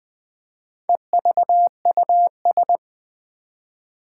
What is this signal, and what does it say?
Morse code, heard as a keyed tone, about 20 wpm, 700 Hz. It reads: EVUS
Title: Four Challenging Letter-Number Combinations 20wpm